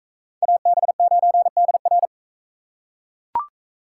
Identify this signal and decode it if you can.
Morse code, heard as a keyed tone, about 42 wpm, 700 Hz. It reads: AB9DR